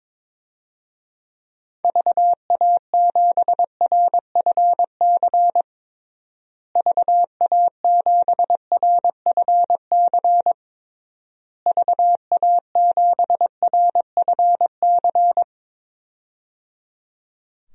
Morse code, heard as a keyed tone, about 22 wpm, 700 Hz. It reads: VA7RFC VA7RFC VA7RFC
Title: DXCC-100 Call Signs - 6 Characters - Repeated 3x - 22wpm